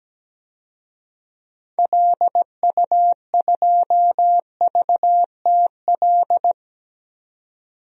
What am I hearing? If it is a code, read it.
Morse code, heard as a keyed tone, about 17 wpm, 700 Hz. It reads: LU2VTL
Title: DXCC-100 Call Signs - 6 Characters - 17wpm